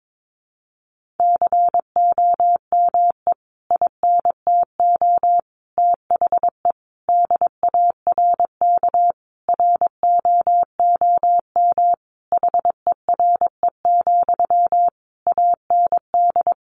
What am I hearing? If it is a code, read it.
Morse code, heard as a keyed tone, about 22 wpm, 700 Hz. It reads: COME INTO THE DARK ROOM HERE, AND
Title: Sentences from Top 1000 Words 22wpm